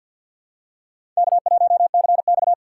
Morse code, heard as a keyed tone, about 48 wpm, 700 Hz. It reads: K1CX
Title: K1USN SST QSO Practice - 50wpm